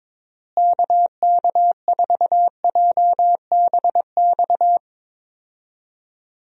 Morse code, heard as a keyed tone, about 22 wpm, 700 Hz. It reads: KK4JBX
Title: USA Call Signs - 6 Characters - Repeated 2x - 22wpm